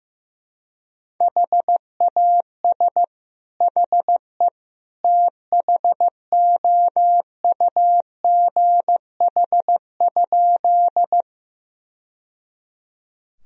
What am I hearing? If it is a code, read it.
Morse code, heard as a keyed tone, about 15 wpm, 700 Hz. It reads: HAS HE THOUGH?
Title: Sentences from Top 1000 Words - Review Part 2 - 15wpm